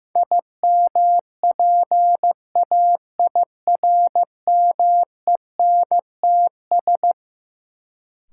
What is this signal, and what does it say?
Morse code, heard as a keyed tone, about 16 wpm, 700 Hz. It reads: IMPAIRMENTS
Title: Sets of 5 Words Encore - New Word Drill - 15wpm